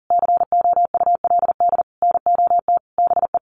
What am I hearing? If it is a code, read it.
Morse code, heard as a keyed tone, about 40 wpm, 700 Hz. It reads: COULD NOT BE
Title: Sentences from Top 900 Words 40wpm